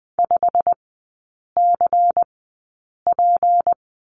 Morse code, heard as a keyed tone, about 20 wpm, spaced as for 8 wpm, 700 Hz. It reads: 5CP
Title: 198 Lesson 32e - Letter k - Three Characters 20wpm